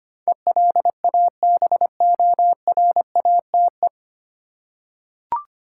Morse code, heard as a keyed tone, about 25 wpm, 700 Hz. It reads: ELABORATE
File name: Nine Letter Words - 25wpm